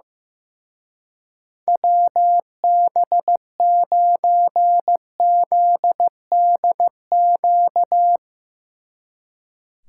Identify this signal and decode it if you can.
Morse code, heard as a keyed tone, about 16 wpm, 700 Hz. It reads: WB9ZDQ
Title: Speed Racing - USA Call Signs 15wpm